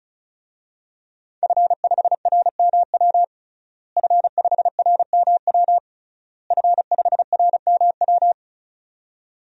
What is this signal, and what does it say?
Morse code, heard as a keyed tone, about 36 wpm, 700 Hz. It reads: F5RMW F5RMW F5RMW
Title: DXCC-100 Call Signs - 5 Characters - Repeated 3x - 35wpm